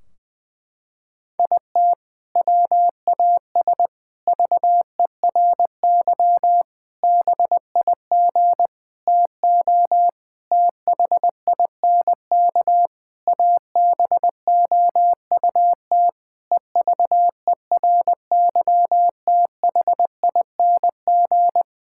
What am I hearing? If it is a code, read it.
Morse code, heard as a keyed tone, about 21 wpm, 700 Hz. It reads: IT WAS VERY BIG TO THINK ABOUT EVERYTHING
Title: Sentences from Top 1600 Words 20wpm